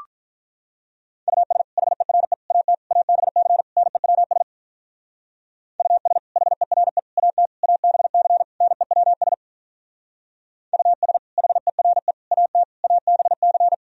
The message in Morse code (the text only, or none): US HERE AT ABC NEWS US HERE AT ABC NEWS US HERE AT ABC